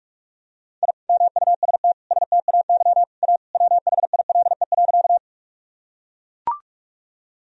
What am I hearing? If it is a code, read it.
Morse code, heard as a keyed tone, about 44 wpm, 700 Hz. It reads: I MUST STAY A WHILE.
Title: Sentences from Top 1000 Words - Review Part 1 - 45wpm